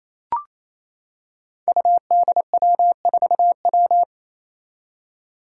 Morse code, heard as a keyed tone, about 29 wpm, 700 Hz. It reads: UDW4W
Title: Five Challenging Letter-Number Combinations 28wpm